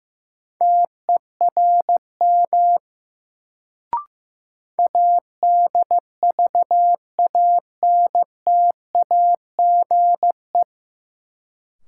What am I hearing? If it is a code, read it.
Morse code, heard as a keyed tone, about 16 wpm, 700 Hz. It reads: TERM ADVANTAGE
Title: Sentences from Top 900 Words - New Word Drill - 15wpm